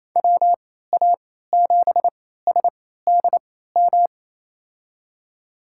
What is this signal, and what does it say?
Morse code, heard as a keyed tone, about 28 wpm, spaced as for 16 wpm, 700 Hz. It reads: WA7SDM
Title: USA Call Signs - 6 Characters - 3x Character Spacing - 28wpm